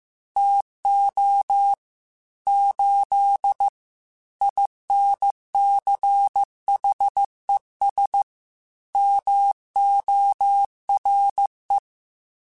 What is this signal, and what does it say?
Morse code, heard as a keyed tone, about 15 wpm, 800 Hz. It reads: TO 8 INCHES MORE
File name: QSOs / QSOs-mp3s / WA7QOX 1@13